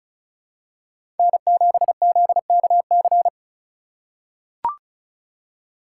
Morse code, heard as a keyed tone, about 34 wpm, 700 Hz. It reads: N7ZKC
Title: USA Call Signs - 5 Characters - 2x Character Spacing - 35wpm